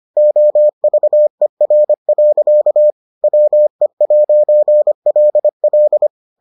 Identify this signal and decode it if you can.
Morse code, heard as a keyed tone, about 25 wpm, 600 Hz. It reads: OVER. WE'LL